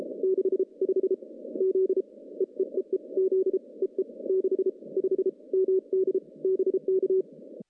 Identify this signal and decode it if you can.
Morse code, heard as a keyed tone, about 33 wpm, 410 Hz. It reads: 6 5 Z EEEE Z EE 6 5 MD BK
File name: contests / CW-SS-k5tr- -0200